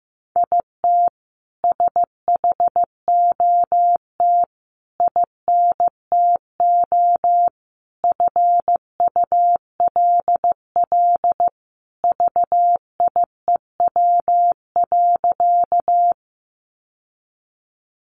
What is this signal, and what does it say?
Morse code, heard as a keyed tone, about 15 wpm, 700 Hz. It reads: IT SHOT INTO FULL VIEW.
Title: Sentences from Top 1100 Words 15wpm